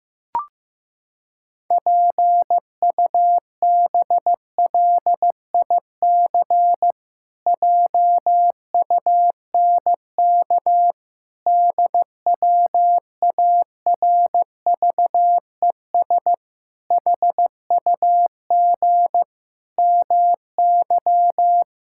Morse code, heard as a keyed tone, about 16 wpm, 700 Hz. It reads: PUBLIC JUNK DWARVES HUG MY
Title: Pangram Challenge v2 - 15wpm